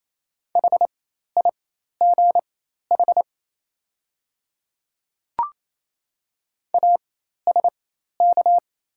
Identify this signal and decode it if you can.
Morse code, heard as a keyed tone, about 27 wpm, spaced as for 13 wpm, 700 Hz. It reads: HIGH ASK
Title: 4x Character Spacing - Top 200 Words - 28wpm